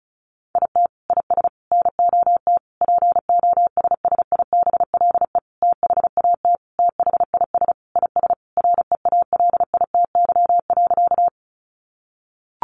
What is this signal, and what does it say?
Morse code, heard as a keyed tone, about 34 wpm, 700 Hz. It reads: IT IS NOT POSSIBLE THAT THIS IS REALITY.
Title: Sentences from Top 1200 Words 35wpm